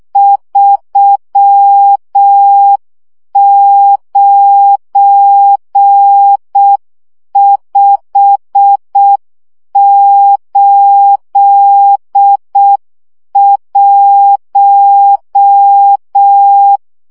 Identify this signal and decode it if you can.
Morse code, heard as a keyed tone, about 6 wpm, 800 Hz. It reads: 39581
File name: morze vizsga 2011-2 5